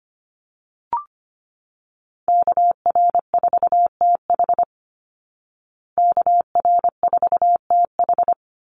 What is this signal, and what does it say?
Morse code, heard as a keyed tone, about 25 wpm, 700 Hz. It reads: KR4TH KR4TH
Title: Call Signs Repeated 3x - 25wpm